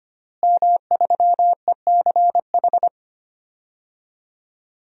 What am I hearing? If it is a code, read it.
Morse code, heard as a keyed tone, about 25 wpm, 700 Hz. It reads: M3ECH
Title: DXCC-100 Call Signs - 5 Characters - Repeated 2x - 25wpm